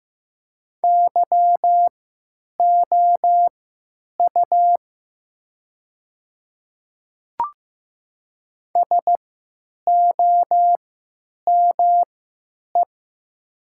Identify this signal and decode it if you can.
Morse code, heard as a keyed tone, about 15 wpm, spaced as for 8 wpm, 690 Hz. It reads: YOU SOME
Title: 3x Character Spacing - Top 100 Words - 15wpm